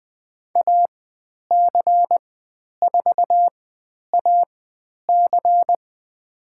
Morse code, heard as a keyed tone, about 20 wpm, spaced as for 10 wpm, 700 Hz. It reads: AC4AC
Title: 109 Lesson 18g - Letter m - Call Signs 20wpm